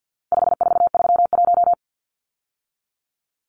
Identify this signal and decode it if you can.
Morse code, heard as a keyed tone, about 50 wpm, 700 Hz. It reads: 5431
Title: Numbers - 4-Digit 50wpm